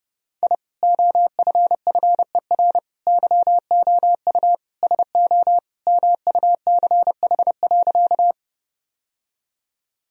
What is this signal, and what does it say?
Morse code, heard as a keyed tone, about 32 wpm, 700 Hz. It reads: I OFFER YOU SO MUCH.
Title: Sentences from Top 1000 Words - Review Part 4 - 30wpm